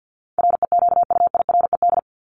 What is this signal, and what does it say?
Morse code, heard as a keyed tone, about 50 wpm, 700 Hz. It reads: REQUIRED
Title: ED Suffix - Top 100 Words with Base Words 50wpm